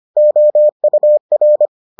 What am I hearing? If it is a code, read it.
Morse code, heard as a keyed tone, about 25 wpm, 600 Hz. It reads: OUR